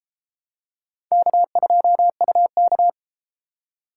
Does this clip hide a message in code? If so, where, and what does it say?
Yes: Morse code from 1.1 to 2.9 s, heard as a keyed tone, about 32 wpm, 700 Hz. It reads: K2UK